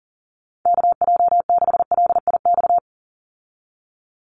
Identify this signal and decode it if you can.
Morse code, heard as a keyed tone, about 40 wpm, 700 Hz. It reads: KJ6LIX
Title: USA Call Signs 40wpm